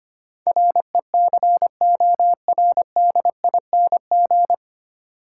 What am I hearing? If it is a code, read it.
Morse code, heard as a keyed tone, about 25 wpm, 700 Hz. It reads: RECORDING
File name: Nine Letter Words - 25wpm